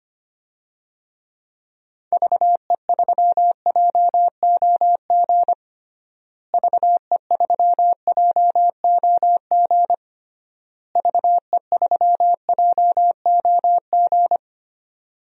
Morse code, heard as a keyed tone, about 25 wpm, 700 Hz. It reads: VE3JOG VE3JOG VE3JOG
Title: DXCC-100 Call Signs - 6 Characters - Repeated 3x - 25wpm